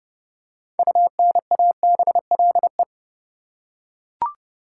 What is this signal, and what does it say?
Morse code, heard as a keyed tone, about 30 wpm, 700 Hz. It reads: UNABLE
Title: Six Letter Words - 30wpm